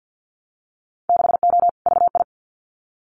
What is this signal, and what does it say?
Morse code, heard as a keyed tone, about 50 wpm, 700 Hz. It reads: 6O VI
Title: Winter Field Day Exchanges Encore 50wpm